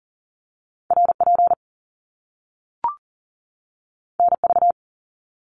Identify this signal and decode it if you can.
Morse code, heard as a keyed tone, about 40 wpm, 700 Hz. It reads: RP NV
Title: Top Two Letter Combinations in Words 40wpm